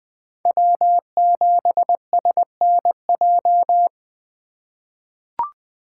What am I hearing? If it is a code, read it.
Morse code, heard as a keyed tone, about 21 wpm, 700 Hz. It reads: W7SNJ
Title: USA Call Signs - 5 Characters - 3x Character Spacing - 20wpm